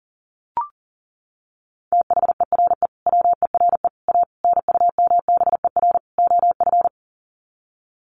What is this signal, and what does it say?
Morse code, heard as a keyed tone, about 38 wpm, 700 Hz. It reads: THERE WERE A NUMBER OF
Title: Sets of 5 Words Encore2 40wpm